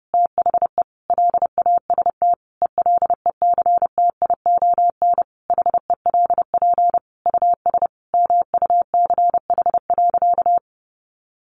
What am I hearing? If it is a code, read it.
Morse code, heard as a keyed tone, about 30 wpm, 700 Hz. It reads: THE LAST ELECTION HELP US MUCH.
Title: Sentences from Top 1500 Words 30wpm